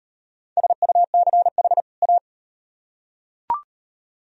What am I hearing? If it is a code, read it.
Morse code, heard as a keyed tone, about 38 wpm, 700 Hz. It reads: SUCH A